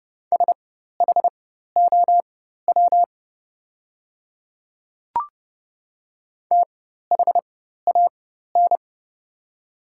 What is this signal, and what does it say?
Morse code, heard as a keyed tone, about 30 wpm, spaced as for 14 wpm, 700 Hz. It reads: SHOW THAN
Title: 4x Character Spacing - Top 300 Words - 30wpm